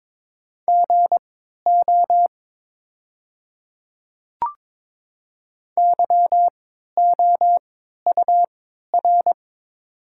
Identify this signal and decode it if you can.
Morse code, heard as a keyed tone, about 23 wpm, spaced as for 13 wpm, 690 Hz. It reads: GO YOUR
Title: 3x Character Spacing - Top 100 Words - 22wpm